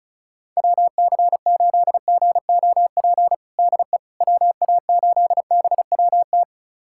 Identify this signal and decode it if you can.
Morse code, heard as a keyed tone, about 34 wpm, 700 Hz. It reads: WC8GOP DE WA8BWT
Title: USA Callsign DE Callsign 35wpm